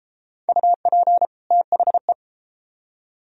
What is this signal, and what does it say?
Morse code, heard as a keyed tone, about 31 wpm, 700 Hz. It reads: UP THE